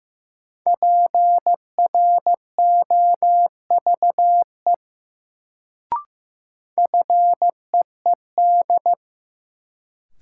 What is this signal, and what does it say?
Morse code, heard as a keyed tone, about 15 wpm, 700 Hz. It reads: PROVE FEED
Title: Sentences from Top 1600 Words - New Word Drill - 15wpm